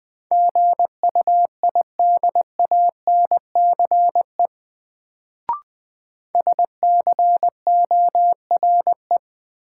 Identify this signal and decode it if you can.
Morse code, heard as a keyed tone, about 21 wpm, 700 Hz. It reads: GUIDANCE SCORE
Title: Sentences from Top 1600 Words - New Word Drill - 20wpm